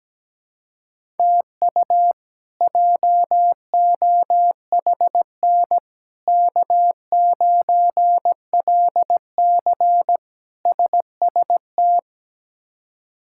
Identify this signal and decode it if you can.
Morse code, heard as a keyed tone, about 17 wpm, 700 Hz. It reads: TU JOHN K9LC SST
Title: K1USN SST QSO Practice - 17wpm